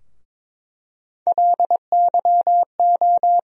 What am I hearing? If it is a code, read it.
Morse code, heard as a keyed tone, about 22 wpm, 700 Hz. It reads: LYO
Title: USA Call Sign Postfixes - 3 Letters 22wpm